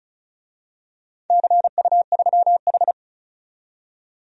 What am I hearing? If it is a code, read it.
Morse code, heard as a keyed tone, about 34 wpm, 700 Hz. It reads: CU3H
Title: DXCC-100 Call Signs - 4 Characters - 35wpm